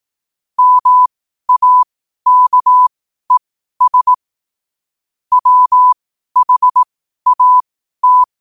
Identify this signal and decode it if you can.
Morse code, heard as a keyed tone, about 18 wpm, spaced as for 12 wpm, 1000 Hz. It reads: MAKES WHAT
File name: qotdfiles / 12wpm